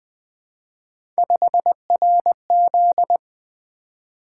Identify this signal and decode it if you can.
Morse code, heard as a keyed tone, about 20 wpm, 700 Hz. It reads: 5RZ
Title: Three Letter-Number Combinations - 20wpm